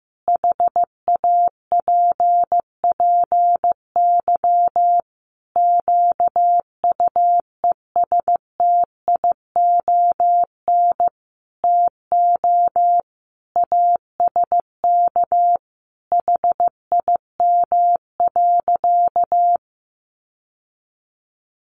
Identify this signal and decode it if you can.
Morse code, heard as a keyed tone, about 15 wpm, 700 Hz. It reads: HAPPY QUESTION TO ASK HIM.
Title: Sentences from Top 1000 Words 15wpm